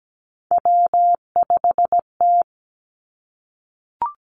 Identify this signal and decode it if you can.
Morse code, heard as a keyed tone, about 17 wpm, 700 Hz. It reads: W5T
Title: Three Challenging Letter-Number Combinations 17wpm